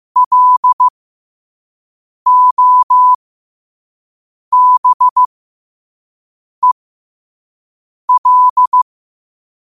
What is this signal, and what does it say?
Morse code, heard as a keyed tone, about 15 wpm, spaced as for 5 wpm, 1000 Hz. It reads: LOBEL